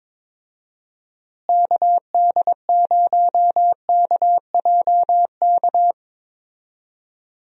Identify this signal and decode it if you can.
Morse code, heard as a keyed tone, about 22 wpm, 700 Hz. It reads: KD0KJK
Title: USA Call Signs - 6 Characters - Repeated 3x - 22wpm